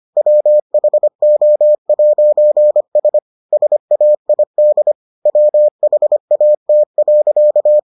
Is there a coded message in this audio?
Yes: Morse code from 0.2 to 7.8 s, heard as a keyed tone, about 25 wpm, 600 Hz. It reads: WHO'S SAID WHAT.